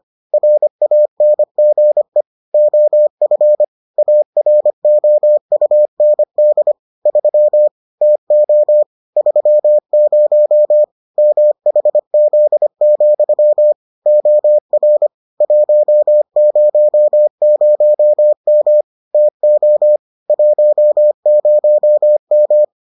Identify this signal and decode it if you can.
Morse code, heard as a keyed tone, about 25 wpm, 600 Hz. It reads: RANGE OF AROUND 3 TO 30 MHZ, OR 100M TO 10M